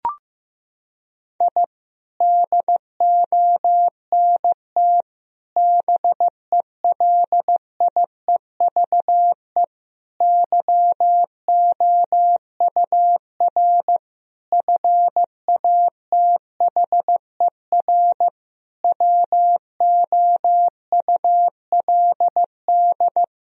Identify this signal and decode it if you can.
Morse code, heard as a keyed tone, about 15 wpm, 700 Hz. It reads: I DONT BELIEVE YOUR FATHER WOULD
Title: Sentences from Top 1000 Words - Review Part 3 - 15wpm